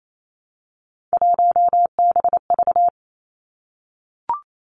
Morse code, heard as a keyed tone, about 28 wpm, 700 Hz. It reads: 1BV
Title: Three Challenging Letter-Number Combinations 28wpm